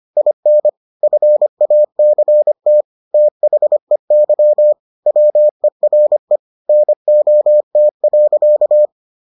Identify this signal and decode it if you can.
Morse code, heard as a keyed tone, about 25 wpm, 600 Hz. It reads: IN FACT THEY WERE NOT.